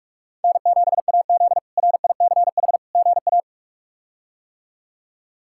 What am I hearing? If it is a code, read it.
Morse code, heard as a keyed tone, about 46 wpm, 700 Hz. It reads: N7AZ RICH GA
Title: K1USN SST QSO Practice - 45wpm